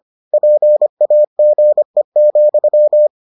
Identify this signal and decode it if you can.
Morse code, heard as a keyed tone, about 25 wpm, 600 Hz. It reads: PAGE,